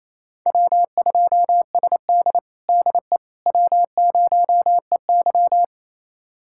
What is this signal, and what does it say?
Morse code, heard as a keyed tone, about 28 wpm, 690 Hz. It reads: W2SD DE W0EY
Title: USA Callsign DE Callsign - 4 Characters - 28wpm